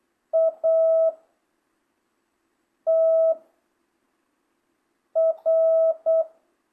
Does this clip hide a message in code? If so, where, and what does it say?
Yes: Morse code from 0.3 to 6.2 s, heard as a keyed tone, about 8 wpm, 660 Hz. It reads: A T R